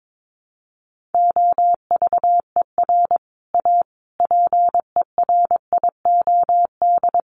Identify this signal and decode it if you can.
Morse code, heard as a keyed tone, about 22 wpm, 700 Hz. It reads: OVER A PERIOD
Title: Sets of 5 Words Encore2 22wpm